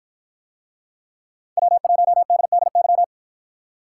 Morse code, heard as a keyed tone, about 53 wpm, 700 Hz. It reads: W1DDY